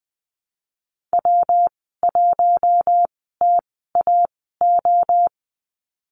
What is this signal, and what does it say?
Morse code, heard as a keyed tone, about 20 wpm, spaced as for 14 wpm, 700 Hz. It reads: W1TAO
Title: 193 Lesson 31g - Letter v - Call Signs 20wpm